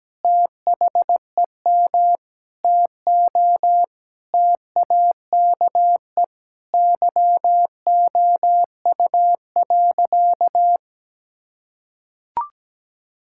Morse code, heard as a keyed tone, about 17 wpm, 700 Hz. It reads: THEM TO TAKE YOU.
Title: Sentences from Top 1000 Words - Review Part 5 - 17wpm